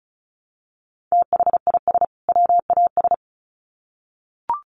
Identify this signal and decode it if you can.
Morse code, heard as a keyed tone, about 34 wpm, 700 Hz. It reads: THIS WAS
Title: Sets of 2 Words 35wpm